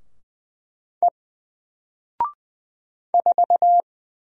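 Morse code, heard as a keyed tone, about 20 wpm, 700 Hz. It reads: E 4
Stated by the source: Single letter and number Morse drill at 20 wpm, from N4LCD.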